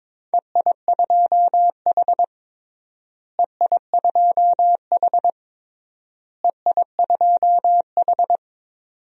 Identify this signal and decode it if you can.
Morse code, heard as a keyed tone, about 21 wpm, 700 Hz. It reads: EI2H EI2H EI2H
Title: DXCC-100 Call Signs - 4 Characters - Repeated 3x - 22wpm